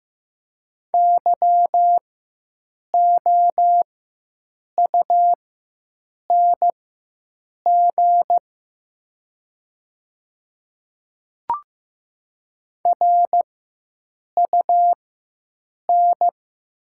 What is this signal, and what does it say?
Morse code, heard as a keyed tone, about 15 wpm, spaced as for 7 wpm, 700 Hz. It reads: YOUNG RUN
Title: 4x Character Spacing - Top 500 Words - 15wpm